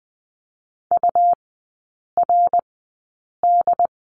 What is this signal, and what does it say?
Morse code, heard as a keyed tone, about 20 wpm, spaced as for 8 wpm, 700 Hz. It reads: URD
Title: 121 Lesson 20e - Number 3 - Three Characters 20wpm